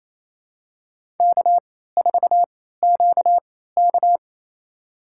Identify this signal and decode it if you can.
Morse code, heard as a keyed tone, about 28 wpm, spaced as for 16 wpm, 700 Hz. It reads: K4QK
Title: USA Call Signs - 4 Characters - 3x Character Spacing - 28wpm